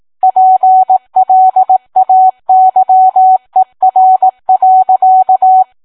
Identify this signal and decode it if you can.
Morse code, heard as a keyed tone, about 18 wpm, 750 Hz. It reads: PLAYER.